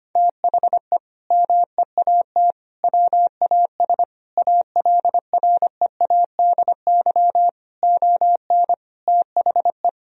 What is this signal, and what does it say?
Morse code, heard as a keyed tone, about 25 wpm, 700 Hz. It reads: THE MEAT WAS ALREADY ON THE
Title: Sentences from Top 1000 Words - Review Part 3 - 25wpm